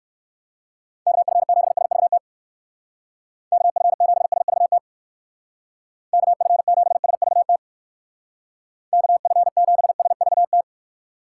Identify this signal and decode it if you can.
Morse code, heard as a keyed tone, about 59 wpm, 700 Hz. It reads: KF7SVT KF7SVT KF7SVT KF7SVT